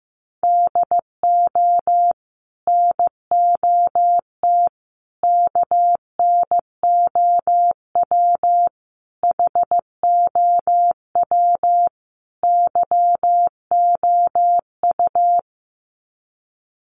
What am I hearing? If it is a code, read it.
Morse code, heard as a keyed tone, about 15 wpm, 700 Hz. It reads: DO NOT KNOW HOW YOU
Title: Sets of 5 Words Encore 15wpm